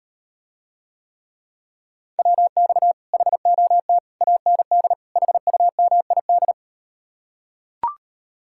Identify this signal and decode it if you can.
Morse code, heard as a keyed tone, about 39 wpm, 700 Hz. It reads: WX HOT AND HUMID